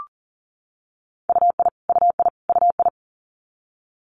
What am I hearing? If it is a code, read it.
Morse code, heard as a keyed tone, about 42 wpm, 700 Hz. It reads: UI UI UI
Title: Multi-letter Phonemes Repeated 3x 40wpm